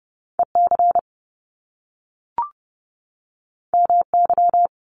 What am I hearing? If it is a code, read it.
Morse code, heard as a keyed tone, about 30 wpm, 700 Hz. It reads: EC MY
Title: Top Two Letter Combinations in Words 30wpm